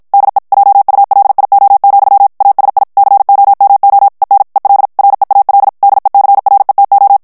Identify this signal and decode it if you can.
Morse code, heard as a keyed tone, about 53 wpm, 800 Hz. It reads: DE JULIO, ASI COMO EN EL REAL DECRETO
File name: morse mp3 / ESPAGNOL TEXTE EN CLAIR 2 VITESSE 50 MOTS MINUTES